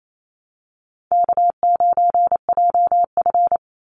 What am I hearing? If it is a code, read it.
Morse code, heard as a keyed tone, about 28 wpm, 700 Hz. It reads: K9JF
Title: Easy Call Signs Repeated 3x - 28wpm